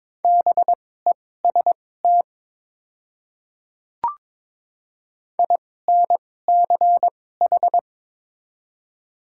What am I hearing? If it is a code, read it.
Morse code, heard as a keyed tone, about 22 wpm, spaced as for 16 wpm, 700 Hz. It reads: BEST INCH
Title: 2x Character Spacing - Top 500 Words - 22wpm